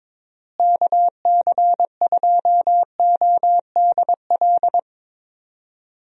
Morse code, heard as a keyed tone, about 22 wpm, 700 Hz. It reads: KC2ODL
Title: USA Call Signs - 6 Characters - Repeated 2x - 22wpm